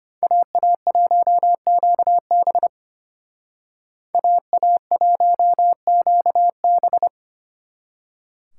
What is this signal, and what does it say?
Morse code, heard as a keyed tone, about 32 wpm, 700 Hz. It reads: AA1QB AA1QB